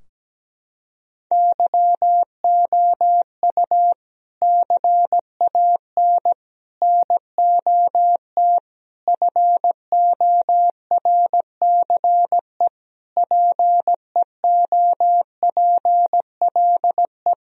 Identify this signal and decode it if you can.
Morse code, heard as a keyed tone, about 17 wpm, 700 Hz. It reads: YOU CAN NOT FORCE PEOPLE
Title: Sentences from Top 1000 Words - Review Part 3 - 17wpm